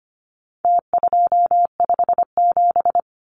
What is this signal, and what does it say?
Morse code, heard as a keyed tone, about 25 wpm, 700 Hz. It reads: T257
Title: Numbers - Contest Style - 4-Digit 25wpm